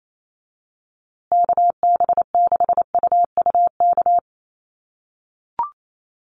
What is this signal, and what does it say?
Morse code, heard as a keyed tone, about 28 wpm, 700 Hz. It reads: KB6UUK
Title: USA Call Signs Repeated 3x 28wpm